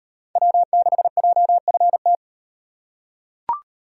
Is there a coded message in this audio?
Yes: Morse code from 0.4 to 2.2 s, heard as a keyed tone, about 37 wpm, 700 Hz. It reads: W6JFT